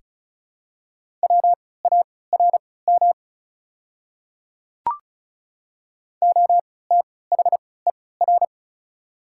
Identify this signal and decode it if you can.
Morse code, heard as a keyed tone, about 34 wpm, spaced as for 20 wpm, 690 Hz. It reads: WARM OTHER
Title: 3x Character Spacing - Top 500 Words - 35wpm